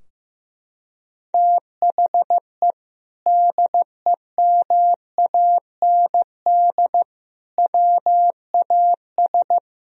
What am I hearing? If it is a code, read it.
Morse code, heard as a keyed tone, about 15 wpm, 700 Hz. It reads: THE DEMAND WAS
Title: Sentences from Top 1000 Words - Review Part 4 - 15wpm